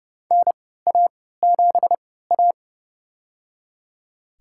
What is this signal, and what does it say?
Morse code, heard as a keyed tone, about 30 wpm, spaced as for 17 wpm, 700 Hz. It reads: NA7A
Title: USA Call Signs - 4 Characters - 3x Character Spacing - 30wpm